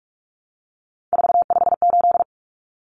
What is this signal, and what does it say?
Morse code, heard as a keyed tone, about 44 wpm, 700 Hz. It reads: 458
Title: Common Cut Numbers - 3-Digit 45wpm